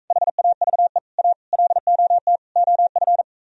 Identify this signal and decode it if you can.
Morse code, heard as a keyed tone, about 45 wpm, 700 Hz. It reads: HAVE A LOT OF